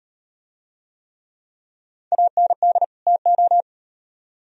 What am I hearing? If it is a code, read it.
Morse code, heard as a keyed tone, about 36 wpm, 700 Hz. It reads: AND TO